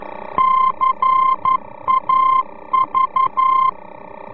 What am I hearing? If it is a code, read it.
Morse code, heard as a keyed tone, about 11 wpm, 1050 Hz. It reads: CAV
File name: france / mx / CAV-VOR-F